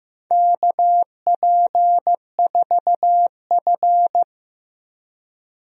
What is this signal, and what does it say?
Morse code, heard as a keyed tone, about 15 wpm, 700 Hz. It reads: KP4F
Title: DXCC-100 Call Signs - 4 Characters - 15wpm